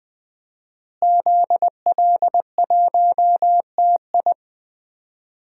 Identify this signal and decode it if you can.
Morse code, heard as a keyed tone, about 20 wpm, 700 Hz. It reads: ZL1TI
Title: DXCC-100 Call Signs - 5 Characters - Repeated 2x - 20wpm